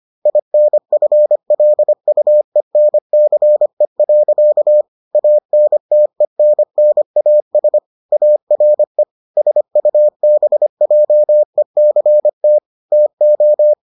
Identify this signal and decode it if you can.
Morse code, heard as a keyed tone, about 25 wpm, 600 Hz. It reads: INFLUENCE. ANTENNAS ARE SUBJECT TO